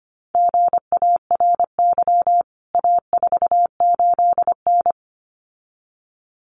Morse code, heard as a keyed tone, about 25 wpm, 700 Hz. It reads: GARY A48N
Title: CWT Exchange 25wpm